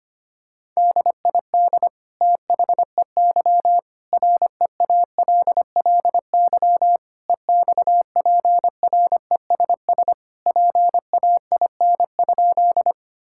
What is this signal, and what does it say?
Morse code, heard as a keyed tone, about 25 wpm, 700 Hz. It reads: DID THEY REALLY EXPRESS PAIN?
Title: Sentences from Top 1600 Words 25wpm